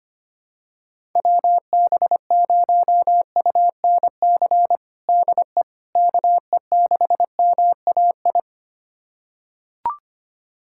Morse code, heard as a keyed tone, about 25 wpm, 700 Hz. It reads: WB0UNC DE KE6MAI
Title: USA Callsign DE Callsign 25wpm